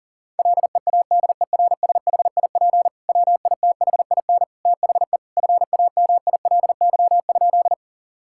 Morse code, heard as a keyed tone, about 40 wpm, 700 Hz. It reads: LEADERSHIP WITHIN THE FAMILY?
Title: Sentences from Top 1600 Words 40wpm